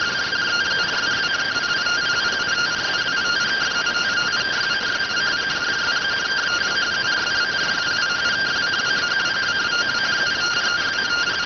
Two-tone teleprinter signal, about 170 Hz shift.